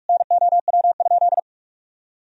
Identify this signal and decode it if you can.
Morse code, heard as a keyed tone, about 46 wpm, 700 Hz. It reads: NOW?
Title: Sentences from Top 1000 Words - Review Part 3 - 45wpm